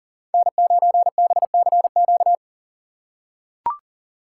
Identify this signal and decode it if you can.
Morse code, heard as a keyed tone, about 40 wpm, 690 Hz. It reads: N9BCQ
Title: USA Call Signs - 5 Characters - 6x Character Spacing - 40wpm